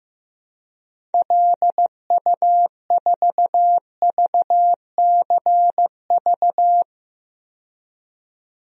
Morse code, heard as a keyed tone, about 16 wpm, 700 Hz. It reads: LU4VCV
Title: DXCC-100 Call Signs - 6 Characters - 15wpm